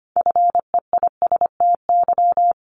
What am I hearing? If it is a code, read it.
Morse code, heard as a keyed tone, about 25 wpm, 700 Hz. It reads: FEISTY
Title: Multi-letter Phonemes 25wpm